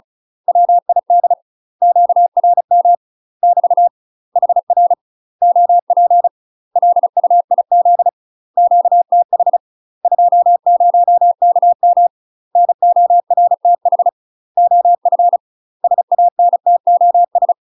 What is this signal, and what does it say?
Morse code, heard as a keyed tone, about 34 wpm, 700 Hz. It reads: WID QRM <BT> HR OP LUIZ QTH 20KM NORTH OF SANTOS
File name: UK MASTER QSOs TRAINING / 35WPM / UK MASTER QSOs 0021